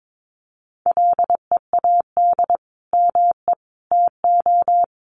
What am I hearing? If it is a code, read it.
Morse code, heard as a keyed tone, about 22 wpm, 700 Hz. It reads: LEAD ME TO
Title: Sentences from Top 500 Words 22wpm